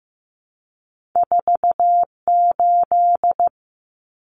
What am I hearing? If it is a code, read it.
Morse code, heard as a keyed tone, about 15 wpm, 700 Hz. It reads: 48
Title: Common Cut Numbers - 2-Digit 15wpm